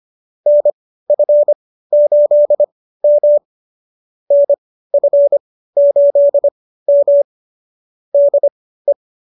Morse code, heard as a keyed tone, about 25 wpm, spaced as for 15 wpm, 590 Hz. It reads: NF8M NF8M DE